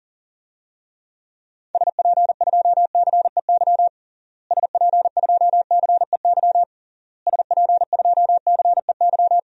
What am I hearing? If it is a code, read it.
Morse code, heard as a keyed tone, about 40 wpm, 700 Hz. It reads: SP2CEY SP2CEY SP2CEY
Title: DXCC-100 Call Signs - 6 Characters - Repeated 3x - 40wpm